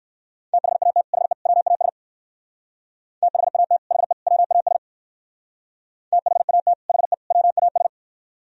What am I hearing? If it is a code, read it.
Morse code, heard as a keyed tone, about 59 wpm, 700 Hz. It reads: THAT HE WAS THAT HE WAS THAT HE WAS